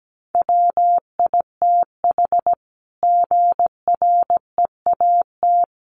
Morse code, heard as a keyed tone, about 17 wpm, 700 Hz. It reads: WITH GREAT
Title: Sentences from Top 900 Words 17wpm